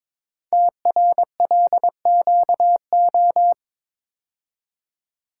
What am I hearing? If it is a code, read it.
Morse code, heard as a keyed tone, about 22 wpm, 700 Hz. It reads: TRLQO
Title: Code Groups - 5 Letter:Numbers - 22wpm